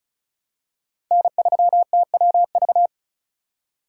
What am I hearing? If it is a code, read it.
Morse code, heard as a keyed tone, about 36 wpm, 700 Hz. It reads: N3TWV
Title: USA Call Signs - 2x Character Spacing - 35wpm